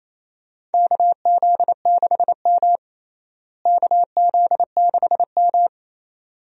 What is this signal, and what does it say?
Morse code, heard as a keyed tone, about 28 wpm, 700 Hz. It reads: KZ6M KZ6M
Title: USA Call Signs - 4 Characters - Repeated 2x - 28wpm